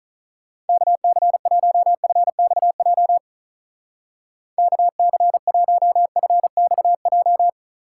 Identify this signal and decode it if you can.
Morse code, heard as a keyed tone, about 45 wpm, 700 Hz. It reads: KC1FXJ KC1FXJ